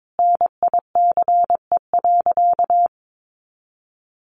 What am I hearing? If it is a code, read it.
Morse code, heard as a keyed tone, about 22 wpm, 700 Hz. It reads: NICE.
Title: Easy Phrases 22wpm